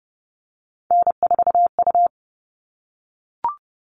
Morse code, heard as a keyed tone, about 30 wpm, 700 Hz. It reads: N4U
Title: All Cut Numbers - 3-Digit 30wpm